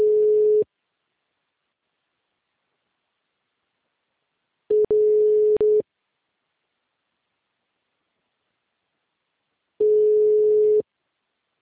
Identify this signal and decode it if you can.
Ringback tone (Europe, 425 Hz).
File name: out- -2057- - - .47351